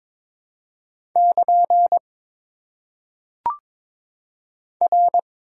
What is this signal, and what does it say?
Morse code, heard as a keyed tone, about 21 wpm, 690 Hz. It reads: <KN> R
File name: Top 90 QSO Elements - 4x Character Spacing - 22wpm